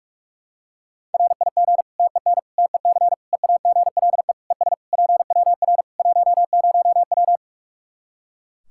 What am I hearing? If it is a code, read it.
Morse code, heard as a keyed tone, about 41 wpm, 700 Hz. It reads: RIG TEN TEC EAGLE ES PWR 10W
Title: K1USN SST QSO Practice 45wpm